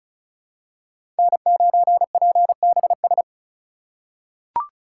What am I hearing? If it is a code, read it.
Morse code, heard as a keyed tone, about 34 wpm, 700 Hz. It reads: N9PBS
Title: USA Call Signs - 5 Characters - 3x Character Spacing - 35wpm